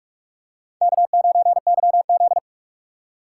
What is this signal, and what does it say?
Morse code, heard as a keyed tone, about 46 wpm, 700 Hz. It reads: K9YZ
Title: USA Call Signs - 4 Characters - 6x Character Spacing - 45wpm